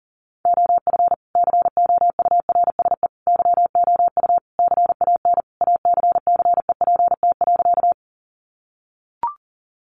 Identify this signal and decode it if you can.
Morse code, heard as a keyed tone, about 40 wpm, 700 Hz. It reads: OF COURSE YOU CAN ACCEPT.
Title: Sentences from Top 1300 Words 40wpm